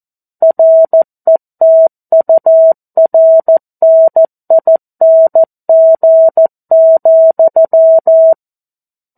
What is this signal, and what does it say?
Morse code, heard as a keyed tone, about 14 wpm, 650 Hz. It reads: RETURNING,
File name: 02-Chapter-02 14 WPM